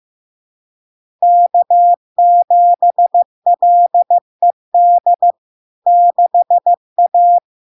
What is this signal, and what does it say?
Morse code, heard as a keyed tone, about 15 wpm, 700 Hz. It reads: K7LED 6A
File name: LARK-FD-2022-Log-in-CW-15wpm-700-Hz